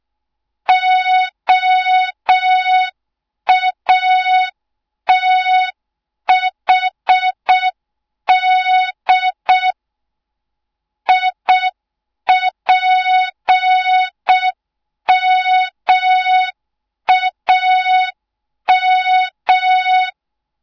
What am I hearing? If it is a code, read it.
Morse code, heard as a keyed tone, about 6 wpm, 3700 Hz. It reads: OATHD IPMAM